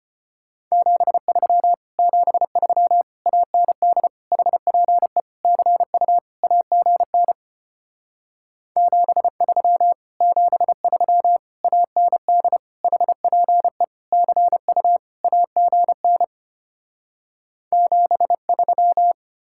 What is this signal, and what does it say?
Morse code, heard as a keyed tone, about 33 wpm, 700 Hz. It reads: 73 73 AND HPE CU AGN 73 73 AND HPE CU AGN 73